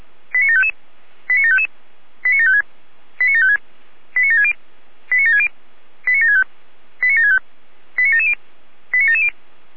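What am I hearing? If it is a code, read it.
Five-tone selective call (ZVEI), heard as quick runs of five tones, tones of about 80 ms. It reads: 89755 89755 89756 89756 89760 89760 89765 89765 89770 89770
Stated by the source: ZVEI five-tone selective call 89751, received on an SDR.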